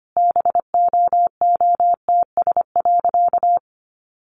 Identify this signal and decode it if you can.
Morse code, heard as a keyed tone, about 25 wpm, 700 Hz. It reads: BOOTS.
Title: A Sound Of Thunder 25wpm